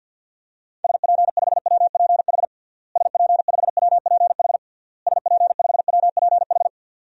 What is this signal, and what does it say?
Morse code, heard as a keyed tone, about 50 wpm, 700 Hz. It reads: SP5WPH SP5WPH SP5WPH
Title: DXCC-100 Call Signs - 6 Characters - Repeated 3x - 50wpm